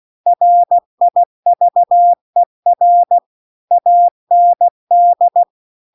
Morse code, heard as a keyed tone, about 16 wpm, 700 Hz. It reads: RIVER AND